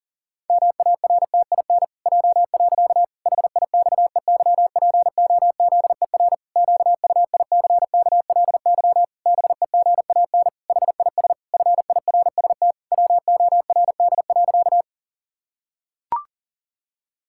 Morse code, heard as a keyed tone, about 37 wpm, 700 Hz. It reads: MARTIN J. HIXEYPOZER QUICKLY BEGAN HIS FIRST WORD.
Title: Pangram Challenge v2 - 40wpm